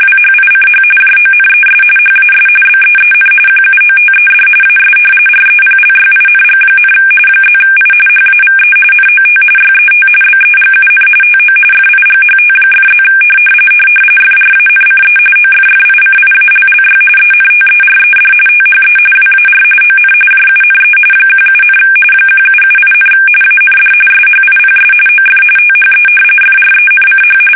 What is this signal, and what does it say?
RTTY, about 45.45 baud. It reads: WHICH JONES HAS BEEN QUOTED AS REPLYING, +I HAVE NOT YET BEGUN TO FIGHT!+ FRANCE OFFICIALLY ENTERED THE WAR ON JUNE 17, 1778, AND SHIPS OF THE FRENCH
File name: fsk850 / usnavy1